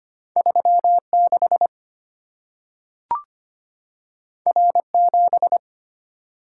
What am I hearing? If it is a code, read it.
Morse code, heard as a keyed tone, about 25 wpm, 690 Hz. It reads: 36 R7
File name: Two Letter-Number Combinations - 25wpm